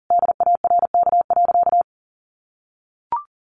Morse code, heard as a keyed tone, about 37 wpm, 700 Hz. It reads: DARK.
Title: Sentences from Top 1100 Words 40wpm